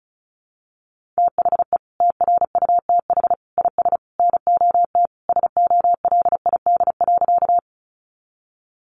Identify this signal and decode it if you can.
Morse code, heard as a keyed tone, about 34 wpm, 700 Hz. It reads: THE TRUTH IS NOT SOLID.
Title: Sentences from Top 1500 Words 35wpm